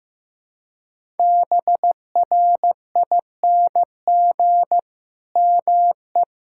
Morse code, heard as a keyed tone, about 15 wpm, 700 Hz. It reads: BRING ME
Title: Sentences from Top 1000 Words - Review Part 6 - 15wpm